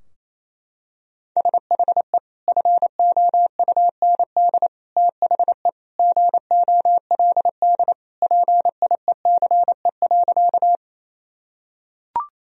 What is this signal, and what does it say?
Morse code, heard as a keyed tone, about 28 wpm, 700 Hz. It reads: SHE FOUND THE GOLD PIECE.
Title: Sentences from Top 1000 Words - Review Part 3 - 28wpm